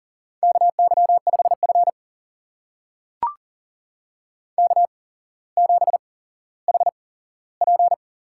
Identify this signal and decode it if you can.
Morse code, heard as a keyed tone, about 40 wpm, spaced as for 17 wpm, 700 Hz. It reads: KY5F K7HP
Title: USA Call Signs - 4 Characters - 8x Character Spacing - 40wpm